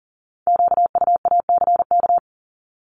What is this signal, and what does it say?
Morse code, heard as a keyed tone, about 39 wpm, 700 Hz. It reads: QUACK
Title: Multi-letter Phonemes 40wpm